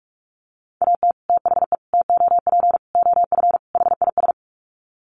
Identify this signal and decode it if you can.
Morse code, heard as a keyed tone, about 44 wpm, 700 Hz. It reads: AT THE TOP OF HIS
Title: Sets of 5 Words Encore 45wpm